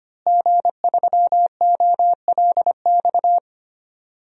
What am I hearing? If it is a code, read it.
Morse code, heard as a keyed tone, about 25 wpm, 700 Hz. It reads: G3OLX
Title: DXCC-100 Call Signs - 5 Characters - Repeated 3x - 25wpm